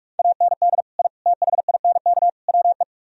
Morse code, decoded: AND I THINK WE

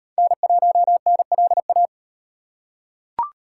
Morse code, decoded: N1NLA

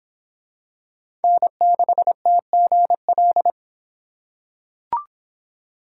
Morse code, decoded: N6TGL

